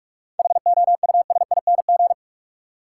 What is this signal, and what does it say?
Morse code, heard as a keyed tone, about 46 wpm, 700 Hz. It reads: HOUSING
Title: Seven Letter Words - 45wpm